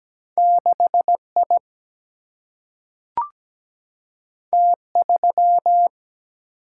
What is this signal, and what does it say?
Morse code, heard as a keyed tone, about 17 wpm, 700 Hz. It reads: 6 I T 3
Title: Single-Letter Single-Number 17wpm